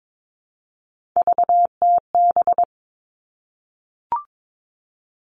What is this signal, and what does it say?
Morse code, heard as a keyed tone, about 22 wpm, 700 Hz. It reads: VTB
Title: All Cut Numbers - 3-Digit 22wpm